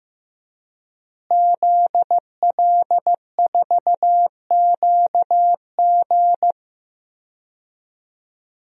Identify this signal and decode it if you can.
Morse code, heard as a keyed tone, about 15 wpm, 700 Hz. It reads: ZL4QG
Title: DXCC-100 Call Signs - 5 Characters - Repeated 2x - 15wpm